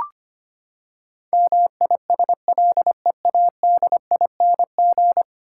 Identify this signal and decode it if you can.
Morse code, heard as a keyed tone, about 25 wpm, 700 Hz. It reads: MISLEADING
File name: Ten Letter Words - 25wpm